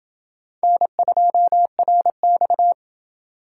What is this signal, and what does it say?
Morse code, heard as a keyed tone, about 28 wpm, 700 Hz. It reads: N2RX